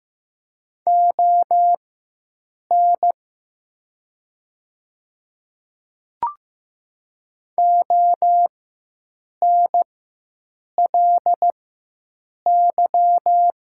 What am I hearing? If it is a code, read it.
Morse code, heard as a keyed tone, about 15 wpm, spaced as for 7 wpm, 700 Hz. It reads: ON ONLY
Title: 4x Character Spacing - Top 300 Words - 15wpm